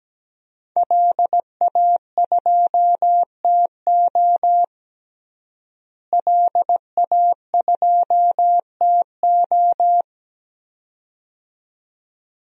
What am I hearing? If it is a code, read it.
Morse code, heard as a keyed tone, about 17 wpm, 700 Hz. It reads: LA2TO LA2TO
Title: DXCC-100 Call Signs - 5 Characters - Repeated 2x - 17wpm